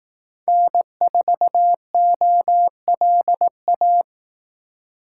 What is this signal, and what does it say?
Morse code, heard as a keyed tone, about 18 wpm, 700 Hz. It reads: N4OLA
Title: Speed Racing - USA Call Signs 15wpm